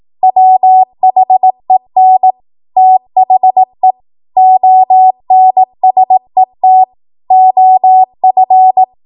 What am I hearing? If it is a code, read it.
Morse code, heard as a keyed tone, about 18 wpm, 750 Hz. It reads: WHEN THE ONSET OF
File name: Archive / 18 WPM / 18WPM